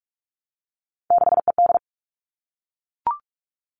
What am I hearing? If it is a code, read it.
Morse code, heard as a keyed tone, about 44 wpm, 700 Hz. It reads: 6ED
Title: All Cut Numbers - 3-Digit 45wpm